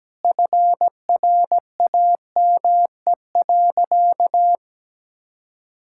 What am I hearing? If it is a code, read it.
Morse code, heard as a keyed tone, about 17 wpm, 700 Hz. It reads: FRAME.
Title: Sentences from Top 1000 Words - Review Part 6 - 17wpm